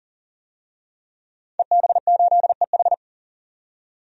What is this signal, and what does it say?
Morse code, heard as a keyed tone, about 40 wpm, 700 Hz. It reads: EB8EH
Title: DXCC-100 Call Signs - 5 Characters - 40wpm